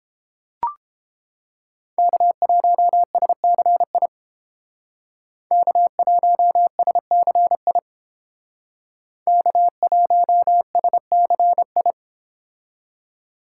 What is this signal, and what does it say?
Morse code, heard as a keyed tone, about 29 wpm, 700 Hz. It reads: K1SCI K1SCI K1SCI